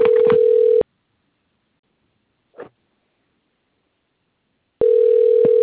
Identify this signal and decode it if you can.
Ringback tone (North America), 440 + 480 Hz.